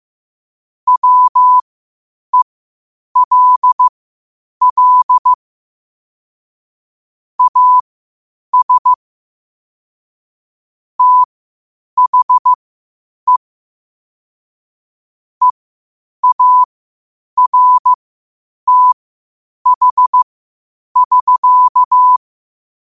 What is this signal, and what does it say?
Morse code, heard as a keyed tone, about 15 wpm, spaced as for 8 wpm, 1000 Hz. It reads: WELL AS THE EARTH<SK>
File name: qotdfiles / 8wpm